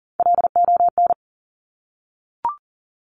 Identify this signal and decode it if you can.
Morse code, heard as a keyed tone, about 39 wpm, 700 Hz. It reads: LON
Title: Top Three Letter Combinations in Words 40wpm